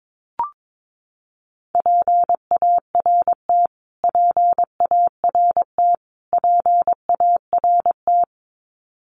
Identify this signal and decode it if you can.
Morse code, heard as a keyed tone, about 22 wpm, 700 Hz. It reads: PART PART PART
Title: Top 500 Words Repeated 22wpm